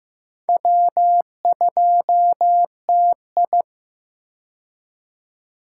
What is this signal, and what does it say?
Morse code, heard as a keyed tone, about 15 wpm, 700 Hz. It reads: W2TI
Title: K1USN SST QSO Practice - 15wpm